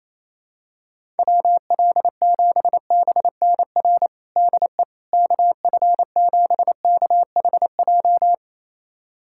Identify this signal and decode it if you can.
Morse code, heard as a keyed tone, about 28 wpm, 700 Hz. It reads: WL7BNR DE KF7KHJ
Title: USA Callsign DE Callsign - 6 Characters - 28wpm